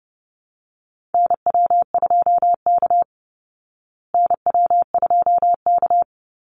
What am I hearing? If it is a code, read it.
Morse code, heard as a keyed tone, about 30 wpm, 700 Hz. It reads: NW2K NW2K
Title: Call Signs Repeated 3x - 30wpm